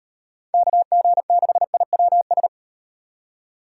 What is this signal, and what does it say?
Morse code, heard as a keyed tone, about 38 wpm, 700 Hz. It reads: KG6IWS